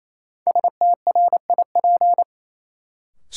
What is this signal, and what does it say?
Morse code, heard as a keyed tone, about 28 wpm, 700 Hz. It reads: STRIP
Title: Sentences from Top 1600 Words - New Word Drill Alt - 28wpm